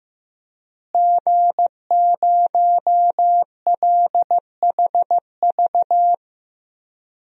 Morse code, heard as a keyed tone, about 15 wpm, 700 Hz. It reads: G0LHV
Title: DXCC-100 Call Signs - 5 Characters - Repeated 3x - 15wpm